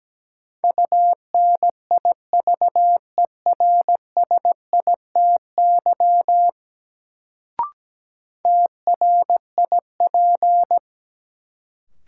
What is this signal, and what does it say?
Morse code, heard as a keyed tone, about 17 wpm, 700 Hz. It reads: UNIVERSITY TRIP
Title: Sentences from Top 1000 Words - Review Part 4 - New Word Drill - 17wpm